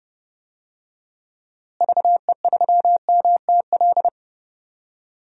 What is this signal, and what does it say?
Morse code, heard as a keyed tone, about 30 wpm, 700 Hz. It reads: VE3MTL
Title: DXCC-100 Call Signs - 6 Characters - Repeated 2x - 30wpm